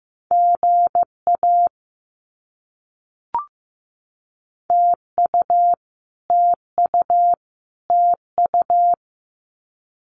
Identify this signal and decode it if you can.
Morse code, heard as a keyed tone, about 15 wpm, 700 Hz. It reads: GA TU TU TU
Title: Abbreviations Repeated 3x - 15wpm